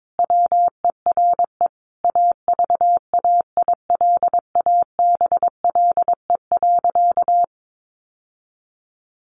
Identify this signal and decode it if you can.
Morse code, heard as a keyed tone, about 22 wpm, 700 Hz. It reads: WERE AVAILABLE.
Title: Sentences from Top 1100 Words 22wpm